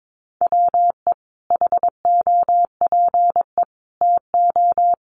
Morse code, heard as a keyed tone, about 22 wpm, 700 Hz. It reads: WE HOPE TO
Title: Sentences from Top 1100 Words 22wpm